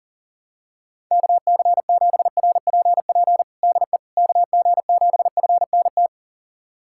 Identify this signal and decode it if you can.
Morse code, heard as a keyed tone, about 37 wpm, 700 Hz. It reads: KC7RPP DE KG7FNT
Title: USA Callsign DE Callsign - 6 Characters - 40wpm